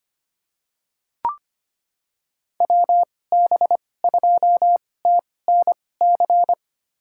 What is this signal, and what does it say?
Morse code, heard as a keyed tone, about 25 wpm, spaced as for 18 wpm, 700 Hz. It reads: WB2TNC
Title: USA Call Signs - 6 Characters - 2x Character Spacing - 25wpm